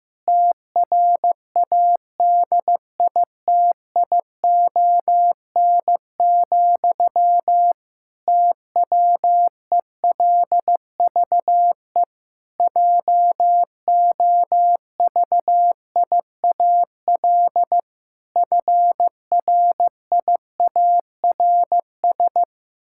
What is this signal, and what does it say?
Morse code, heard as a keyed tone, about 14 wpm, 700 Hz. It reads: TRADITION, TWELVE JOVIAL FRIARS
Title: Pangram Challenge v2 - 15wpm